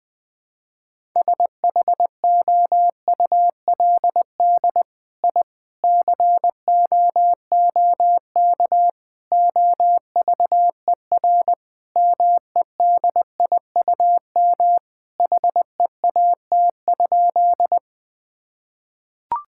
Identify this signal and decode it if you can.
Morse code, heard as a keyed tone, about 21 wpm, 700 Hz. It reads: SHOULD I COOK OVER MEDIUM HEAT?
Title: Sentences from Top 1000 Words - Review Part 6 - 20wpm